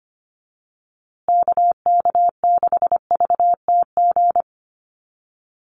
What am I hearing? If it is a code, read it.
Morse code, heard as a keyed tone, about 25 wpm, 700 Hz. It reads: KK6VTG
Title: USA Call Signs Repeated 2x 25wpm